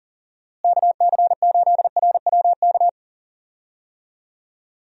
Morse code, heard as a keyed tone, about 38 wpm, 700 Hz. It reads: KC8RWK
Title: USA Call Signs - 6 Characters - Repeated 2x - 40wpm